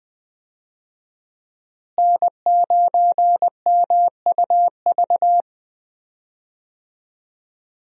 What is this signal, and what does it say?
Morse code, heard as a keyed tone, about 20 wpm, 700 Hz. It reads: N9MUV
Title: USA Call Signs - 5 Characters - Repeated 2x - 20wpm